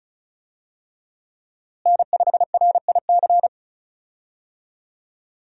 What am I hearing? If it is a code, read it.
Morse code, heard as a keyed tone, about 36 wpm, 700 Hz. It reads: N5RIC
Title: USA Call Signs - 5 Characters - Repeated 3x - 35wpm